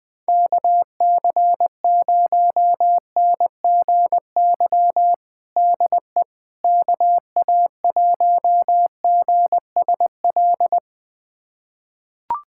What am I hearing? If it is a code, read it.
Morse code, heard as a keyed tone, about 20 wpm, 700 Hz. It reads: KC0NGY DE KA1GSL
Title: USA Callsign DE Callsign - 6 Characters - 20wpm